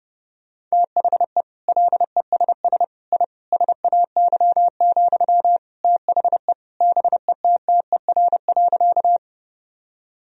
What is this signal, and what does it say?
Morse code, heard as a keyed tone, about 30 wpm, 700 Hz. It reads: THE LESS I SAY, THE BETTER.
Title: Sentences from Top 1000 Words - Review Part 2 - 30wpm